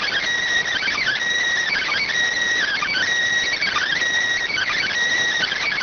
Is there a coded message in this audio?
Yes: Emergency Alert System (SAME) tones from 0.7 to 4.9 s, heard as harsh buzzing bursts.